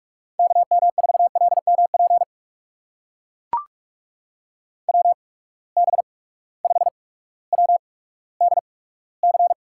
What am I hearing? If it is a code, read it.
Morse code, heard as a keyed tone, about 44 wpm, spaced as for 19 wpm, 700 Hz. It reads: KM4LMP WB5WDC
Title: USA Call Signs - 8x Character Spacing - 45wpm